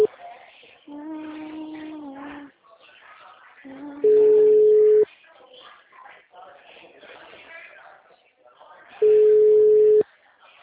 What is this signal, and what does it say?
Ringback tone (Europe, 425 Hz), with speech.
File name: out- -2005- - - .39208